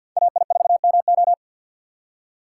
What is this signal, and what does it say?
Morse code, heard as a keyed tone, about 51 wpm, 700 Hz. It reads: AI4MO
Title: USA Call Signs - 5 Characters - 3x Character Spacing - 50wpm